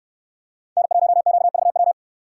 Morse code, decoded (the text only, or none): N1QVW